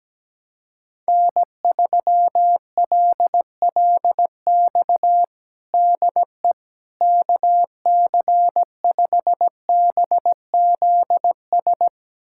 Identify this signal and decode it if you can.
Morse code, heard as a keyed tone, about 17 wpm, 700 Hz. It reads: N3LLX DE KC5BZS
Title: USA Callsign DE Callsign 17wpm